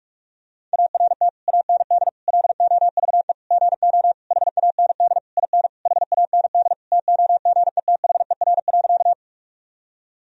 Morse code, heard as a keyed tone, about 42 wpm, 700 Hz. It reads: ART AND LOVE GO HAND IN HAND TOGETHER.
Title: Sentences from Top 1000 Words - Review Part 3 - 45wpm